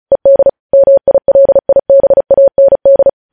Morse code, heard as a keyed tone, about 36 wpm, 550 Hz. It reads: ED MILIBAND
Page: news headlines in morse code at 35 wpm